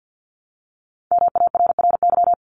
Morse code, heard as a keyed tone, about 50 wpm, 700 Hz. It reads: MARRY
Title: Multi-letter Phonemes Repeated 3x 50wpm